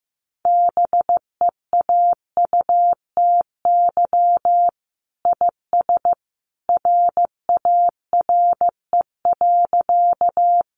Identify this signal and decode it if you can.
Morse code, heard as a keyed tone, about 15 wpm, 700 Hz. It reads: BEAUTY IS RARE.
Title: Sentences from Top 1100 Words 15wpm